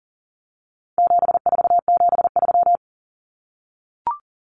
Morse code, heard as a keyed tone, about 40 wpm, 700 Hz. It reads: 7473
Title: Common Cut Numbers - 4-Digit 40wpm